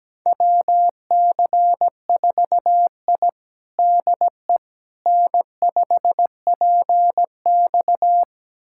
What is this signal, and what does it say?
Morse code, heard as a keyed tone, about 17 wpm, 700 Hz. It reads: WC4I DE N5PX
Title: USA Callsign DE Callsign - 4 Characters - 17wpm